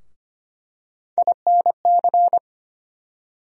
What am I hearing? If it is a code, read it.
Morse code, heard as a keyed tone, about 25 wpm, 700 Hz. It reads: INC
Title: USA Call Sign Postfixes - 3 Letters 25wpm